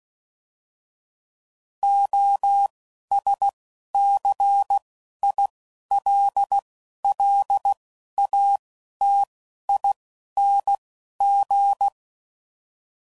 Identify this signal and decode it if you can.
Morse code, heard as a keyed tone, about 16 wpm, 800 Hz. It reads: O S C I L L A T I N G
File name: Suffixes / Special INGs / Single INGs